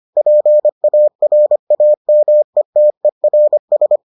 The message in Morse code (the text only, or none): PARAMETERS